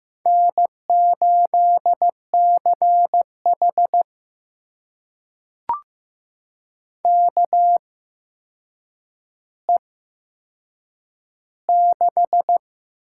Morse code, heard as a keyed tone, about 15 wpm, spaced as for 7 wpm, 700 Hz. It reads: N8CH KE6
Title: USA Call Signs - 8x Character Spacing - 15wpm